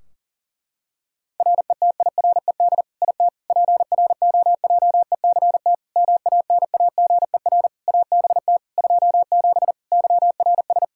Morse code, heard as a keyed tone, about 40 wpm, 700 Hz. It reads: RETIRED IT PROJECT MANAGER ABT 27 YRS
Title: Practice QSO - Exchange 3 - More Info - 40wpm